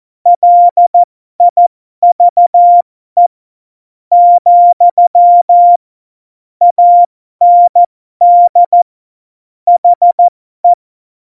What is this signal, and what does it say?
Morse code, heard as a keyed tone, about 14 wpm, spaced as for 11 wpm, 700 Hz. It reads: LIVE, AND HE
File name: Morsekurs / Witze